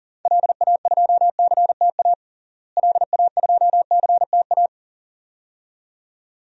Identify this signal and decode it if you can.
Morse code, heard as a keyed tone, about 40 wpm, 700 Hz. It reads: LA2CTA LA2CTA
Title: DXCC-100 Call Signs - 6 Characters - Repeated 2x - 40wpm